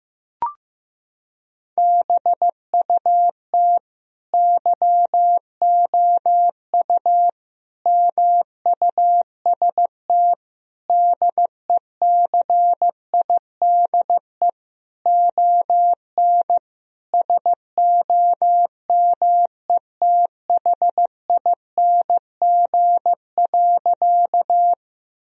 Morse code, heard as a keyed tone, about 15 wpm, 700 Hz. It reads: BUT YOU MUST DECIDE ON SOMETHING.
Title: Sentences from Top 1000 Words - Review Part 3 - 15wpm